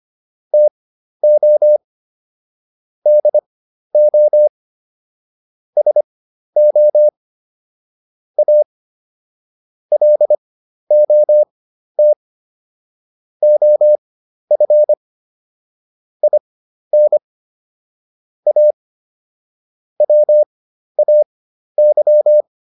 Morse code, heard as a keyed tone, about 25 wpm, spaced as for 12 wpm, 600 Hz. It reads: TO DO SO A LOT OF IN A WAY